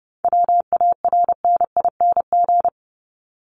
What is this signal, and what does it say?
Morse code, heard as a keyed tone, about 30 wpm, 700 Hz. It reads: WARNING
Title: Top 1751-2000 Words 30wpm